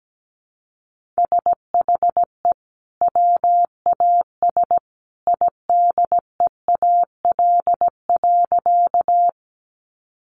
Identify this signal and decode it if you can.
Morse code, heard as a keyed tone, about 17 wpm, 700 Hz. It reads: SHE WAS IDEAL.
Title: Sentences from Top 1400 Words 17wpm